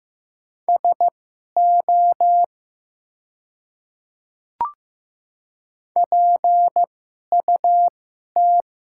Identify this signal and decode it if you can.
Morse code, heard as a keyed tone, about 15 wpm, spaced as for 11 wpm, 700 Hz. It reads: SO PUT
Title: 2x Character Spacing - Top 200 Words - 15wpm